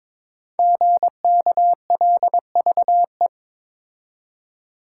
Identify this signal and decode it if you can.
Morse code, heard as a keyed tone, about 22 wpm, 700 Hz. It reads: GKLVE
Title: Code Groups - 5 Letter:Numbers - 22wpm